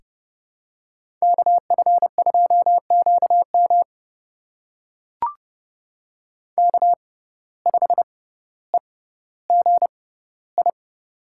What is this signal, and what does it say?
Morse code, heard as a keyed tone, about 30 wpm, spaced as for 15 wpm, 700 Hz. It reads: KF2QM K5EGI
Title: USA Call Signs - 5 Characters - 6x Character Spacing - 30wpm